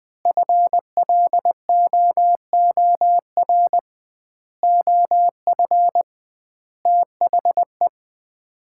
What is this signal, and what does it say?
Morse code, heard as a keyed tone, about 21 wpm, 700 Hz. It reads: FLOOR OF THE
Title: 2x Word-Spacing - Sets of 5 Words - 20wpm